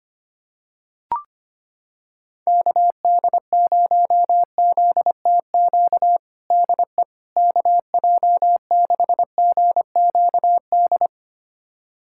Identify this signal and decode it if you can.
Morse code, heard as a keyed tone, about 25 wpm, 700 Hz. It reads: KD0ZTQ DE KJ6GQD
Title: USA Callsign DE Callsign 25wpm